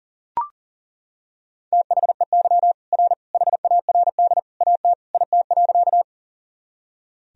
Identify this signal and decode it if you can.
Morse code, heard as a keyed tone, about 37 wpm, 700 Hz. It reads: THEY R HARD AT IT.
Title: Sentences from Top 300 Words and Abbreviations 40wpm